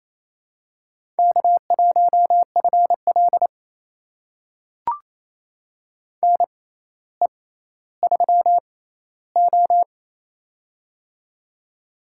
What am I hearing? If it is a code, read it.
Morse code, heard as a keyed tone, about 28 wpm, spaced as for 14 wpm, 700 Hz. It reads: K1FL NE3O
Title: USA Call Signs - 4 Characters - 6x Character Spacing - 28wpm